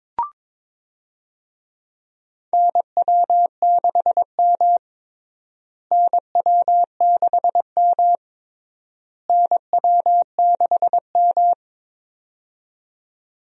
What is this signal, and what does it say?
Morse code, heard as a keyed tone, about 22 wpm, 700 Hz. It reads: NW6M NW6M NW6M
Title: USA Call Signs - 4 Characters - Repeated 3x - 22wpm